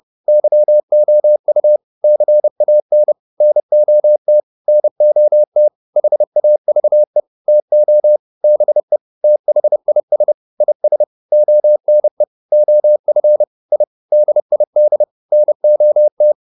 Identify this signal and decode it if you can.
Morse code, heard as a keyed tone, about 30 wpm, 600 Hz. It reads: YOU CAN NOT NOT HAVE TO BE THIS IS ONE OF I DID NOT